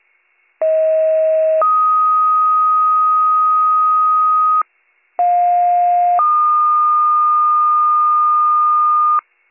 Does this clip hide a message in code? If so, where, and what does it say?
Yes: two-tone page from 0.6 to 9.2 s, heard as a one-second tone, then a three-second tone, about 640 then 1250 Hz, about 710 then 1220 Hz.